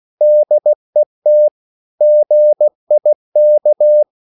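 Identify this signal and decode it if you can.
Morse code, heard as a keyed tone, about 16 wpm, 600 Hz. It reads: DET GIK
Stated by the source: Slow (QRS) Morse code at 16 wpm, from CW med Gnister.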